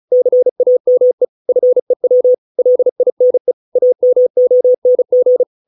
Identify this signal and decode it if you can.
Morse code, heard as a keyed tone, about 35 wpm, 500 Hz. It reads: CAME FEW LINE AMONG